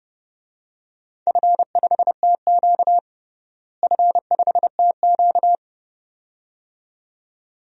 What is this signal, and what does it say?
Morse code, heard as a keyed tone, about 31 wpm, 700 Hz. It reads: F5TQ F5TQ
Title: DXCC-100 Call Signs - 4 Characters - Repeated 2x - 30wpm